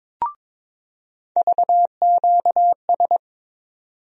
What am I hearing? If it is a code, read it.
Morse code, heard as a keyed tone, about 22 wpm, 700 Hz. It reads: VQS
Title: USA Call Sign Postfixes - 3 Letters 22wpm